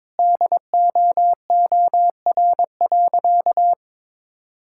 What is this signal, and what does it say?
Morse code, heard as a keyed tone, about 22 wpm, 700 Hz. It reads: DOOR.
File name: practice / 22wpm / Easy Sentences with Hints - 22wpm